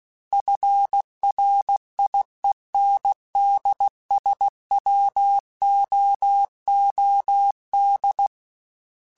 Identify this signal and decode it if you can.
Morse code, heard as a keyed tone, about 16 wpm, 800 Hz. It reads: FRIENDSWOOD
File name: fromSMs / FromSMs mp3s / Cities@15